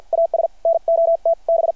Morse code, heard as a keyed tone, about 51 wpm, 660 Hz. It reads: AS NOT B
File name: archived / CRSnet